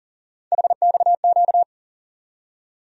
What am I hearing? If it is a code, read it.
Morse code, heard as a keyed tone, about 40 wpm, 700 Hz. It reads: HXQ
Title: USA Call Sign Postfixes - 3 Letters 40wpm